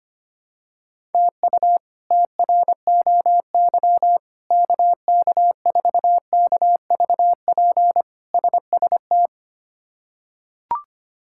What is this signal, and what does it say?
Morse code, heard as a keyed tone, about 25 wpm, 700 Hz. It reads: TU TROY KK4KVP SST
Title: K1USN SST QSO Practice - 25wpm